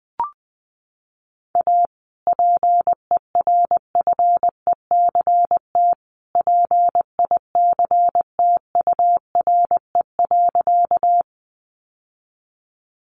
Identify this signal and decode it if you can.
Morse code, heard as a keyed tone, about 20 wpm, 700 Hz. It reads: A PERFECT PICTURE.
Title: Sentences from Top 1000 Words 20wpm